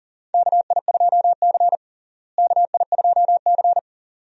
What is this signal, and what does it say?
Morse code, heard as a keyed tone, about 37 wpm, 700 Hz. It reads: KI2C KI2C
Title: USA Call Signs - 4 Characters - Repeated 2x - 40wpm